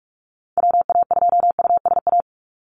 Morse code, heard as a keyed tone, about 44 wpm, 700 Hz. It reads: WA2USA
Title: NA SOTA Chaser Callsigns 45wpm